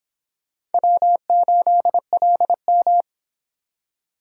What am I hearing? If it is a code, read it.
Morse code, heard as a keyed tone, about 30 wpm, 700 Hz. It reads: W8LM